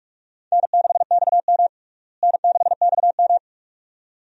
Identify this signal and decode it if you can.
Morse code, heard as a keyed tone, about 44 wpm, 700 Hz. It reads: N6XM N6XM
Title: USA Call Signs - 4 Characters - Repeated 2x - 45wpm